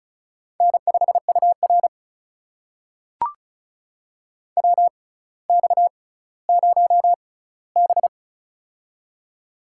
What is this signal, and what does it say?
Morse code, heard as a keyed tone, about 36 wpm, spaced as for 18 wpm, 700 Hz. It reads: N5UR WX0B
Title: USA Call Signs - 4 Characters - 6x Character Spacing - 35wpm